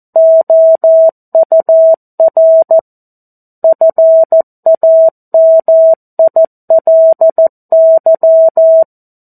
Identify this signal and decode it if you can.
Morse code, heard as a keyed tone, about 14 wpm, 650 Hz. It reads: OUR FAMILY
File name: 02-Chapter-02 14 WPM